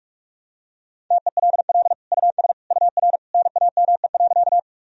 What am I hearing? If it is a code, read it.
Morse code, heard as a keyed tone, about 43 wpm, 700 Hz. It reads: TELL US UR NAME.
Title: Sentences from Top 200 Words and Abbreviations 45wpm